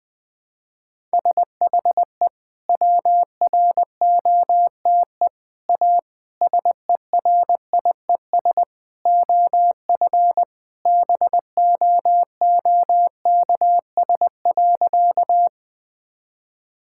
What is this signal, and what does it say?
Morse code, heard as a keyed tone, about 20 wpm, 700 Hz. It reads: SHE WROTE A SERIES OF BOOKS.
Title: Sentences with 3-Word Phrases - 20wpm